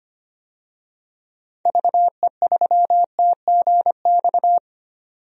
Morse code, heard as a keyed tone, about 25 wpm, 700 Hz. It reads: VE3TGX
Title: DXCC-100 Call Signs - 6 Characters - 25wpm